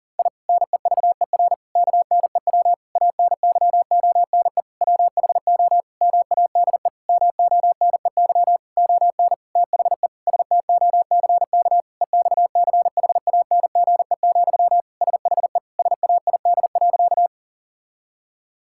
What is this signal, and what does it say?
Morse code, heard as a keyed tone, about 40 wpm, 700 Hz. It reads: I NEVER KNEW ANYONE WHO MADE MONEY ON THE STOCK EXCHANGE, SHE SAID.
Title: Sentences from Top 1000 Words - Review Part 6 - 40wpm